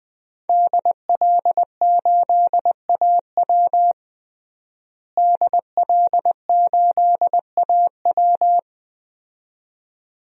Morse code, heard as a keyed tone, about 20 wpm, 700 Hz. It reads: DL8AW DL8AW
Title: DXCC-100 Call Signs - 5 Characters - Repeated 2x - 20wpm